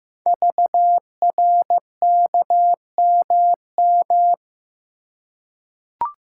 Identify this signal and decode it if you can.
Morse code, heard as a keyed tone, about 15 wpm, 700 Hz. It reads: VRKMM
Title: Code Groups - 5 Letters - 15wpm